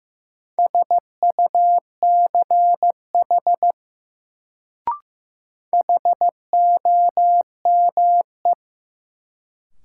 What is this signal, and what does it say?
Morse code, heard as a keyed tone, about 15 wpm, 700 Hz. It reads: SUCH HOME
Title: Sentences from Top 200 Words - New Word Drill - 15wpm